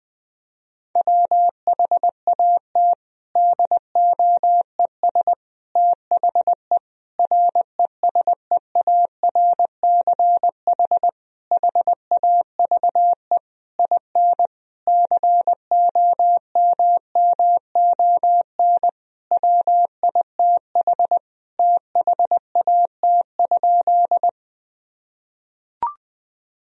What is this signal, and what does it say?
Morse code, heard as a keyed tone, about 19 wpm, 700 Hz. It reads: WHAT DOES THE RESEARCH HAVE IN COMMON WITH THAT?
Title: Sentences from Top 1000 Words - Review Part 4 - 20wpm